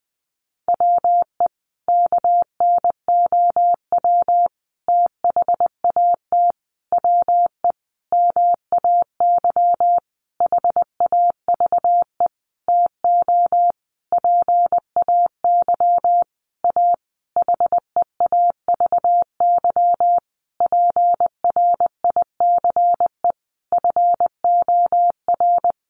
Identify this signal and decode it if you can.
Morse code, heard as a keyed tone, about 20 wpm, 700 Hz. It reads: WE KNOW THAT WE MAY HAVE TO PAY A HEAVY PRICE FOR
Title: Sentences from Top 1100 Words 20wpm